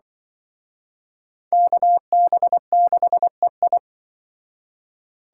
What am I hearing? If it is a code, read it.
Morse code, heard as a keyed tone, about 24 wpm, 700 Hz. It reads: KB6EI
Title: Speed Racing - USA 2x2 Call Signs 20wpm